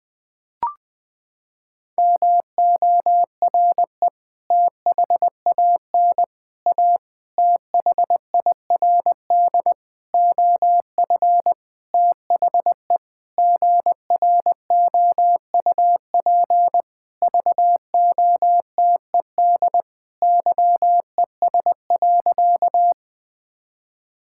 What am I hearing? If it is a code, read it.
Morse code, heard as a keyed tone, about 20 wpm, 700 Hz. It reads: MORE THAN A THIRD OF THE GROUP VOTED YES.
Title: Sentences with 6-Word Phrases - 20wpm